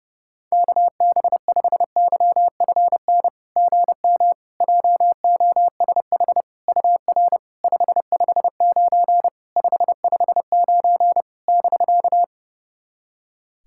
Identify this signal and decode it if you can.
Morse code, heard as a keyed tone, about 28 wpm, 660 Hz. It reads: KB5YFN GM JOSH UR 559 559 <BK>
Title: SOTA QSO Practice 30wpm